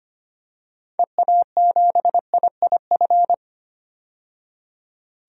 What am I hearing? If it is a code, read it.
Morse code, heard as a keyed tone, about 25 wpm, 700 Hz. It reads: EA7IIF
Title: DXCC-100 Call Signs - 6 Characters - 25wpm